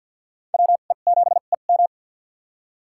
Morse code, heard as a keyed tone, about 53 wpm, spaced as for 37 wpm, 700 Hz. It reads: WE7EM
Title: USA Call Signs - 5 Characters - 2x Character Spacing - 50wpm